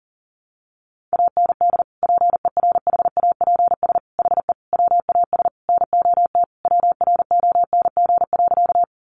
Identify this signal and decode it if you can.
Morse code, heard as a keyed tone, about 40 wpm, 700 Hz. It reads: AND PERHAPS HE WAS NOT WRONG.
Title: Sentences from Top 900 Words 40wpm